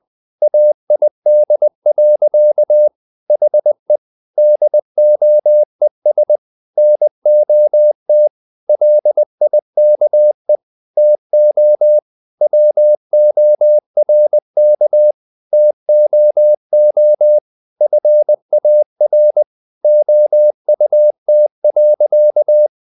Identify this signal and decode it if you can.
Morse code, heard as a keyed tone, about 20 wpm, 600 Hz. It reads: AID. HE DOES NOT LIKE TO WORK TOO FAR OUT.
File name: the old man and the sea / the old man and the sea part 1 20wpm